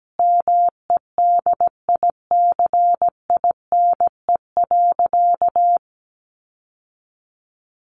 Morse code, heard as a keyed tone, about 17 wpm, 700 Hz. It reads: MEDICINE.
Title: Sentences from Top 900 Words 17wpm